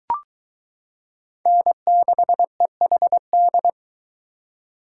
Morse code, heard as a keyed tone, about 23 wpm, 700 Hz. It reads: N6EHD